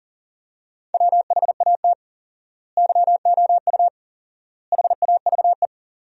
Morse code, heard as a keyed tone, about 37 wpm, 700 Hz. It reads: WHAT YOU HAVE
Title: 4x Word-Spacing - Sentences from Top 200 Words - 40wpm